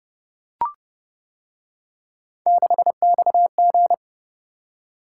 Morse code, heard as a keyed tone, about 29 wpm, 700 Hz. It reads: 6XG